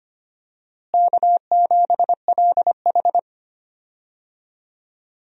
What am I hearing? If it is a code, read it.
Morse code, heard as a keyed tone, about 25 wpm, 700 Hz. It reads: K7LH
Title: USA Call Signs - 4 Characters - Repeated 2x - 25wpm